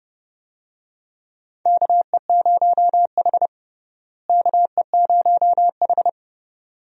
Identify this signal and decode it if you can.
Morse code, heard as a keyed tone, about 30 wpm, 700 Hz. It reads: KE0H KE0H
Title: USA Call Signs - 4 Characters - Repeated 2x - 30wpm